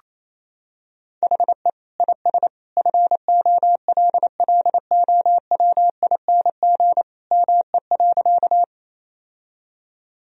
Morse code, heard as a keyed tone, about 28 wpm, 700 Hz. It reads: HE IS FOLLOWING ME.
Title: Sentences from Top 1000 Words - Review Part 4 - 28wpm